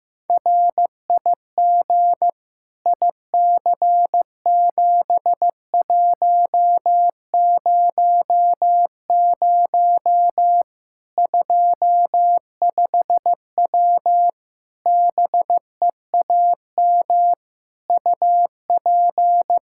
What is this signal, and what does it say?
Morse code, heard as a keyed tone, about 15 wpm, 700 Hz. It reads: RIG IC7100 25W BEAM UP
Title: Practice QSO - Exchange 2 - Info - 15wpm